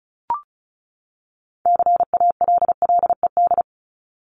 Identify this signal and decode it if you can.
Morse code, heard as a keyed tone, about 34 wpm, 700 Hz. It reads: CALLED
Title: ED Suffix - Top 100 Words with Base Words 35wpm